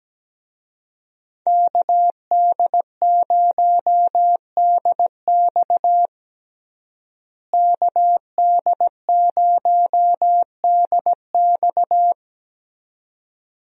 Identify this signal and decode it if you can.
Morse code, heard as a keyed tone, about 17 wpm, 700 Hz. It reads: KD0DX KD0DX
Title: USA Call Signs - 5 Characters - Repeated 2x - 17wpm